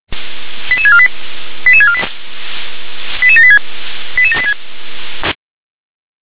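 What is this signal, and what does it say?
Five-tone selective call (ZVEI), heard as quick runs of five tones, tones of about 70 ms. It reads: 88648 88648 88786 88786
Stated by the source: ZVEI five-tone selective call 88913 received by SDR.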